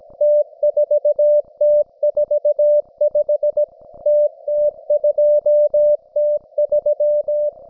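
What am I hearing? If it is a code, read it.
Morse code, heard as a keyed tone, about 17 wpm, 590 Hz. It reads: T4T45 TT2T3
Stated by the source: Russian military Morse station, M32, RDL, on 5268 kHz.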